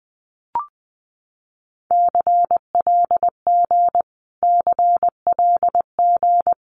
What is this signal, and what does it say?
Morse code, heard as a keyed tone, about 20 wpm, 700 Hz. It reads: CLG CLG
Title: Abbreviations Repeated 3x - 20wpm